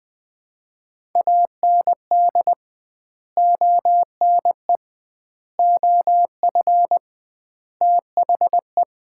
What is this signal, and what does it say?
Morse code, heard as a keyed tone, about 20 wpm, 700 Hz. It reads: AND ONE OF THE
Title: 2x Word-Spacing - Sets of 4 Words - 20wpm